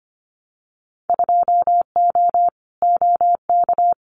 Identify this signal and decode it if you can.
Morse code, heard as a keyed tone, about 25 wpm, 700 Hz. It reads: 2O OK
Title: Winter Field Day Exchanges 25wpm